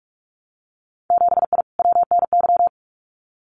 Morse code, heard as a keyed tone, about 46 wpm, 700 Hz. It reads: 7I WNY
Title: Winter Field Day Exchanges Encore 45wpm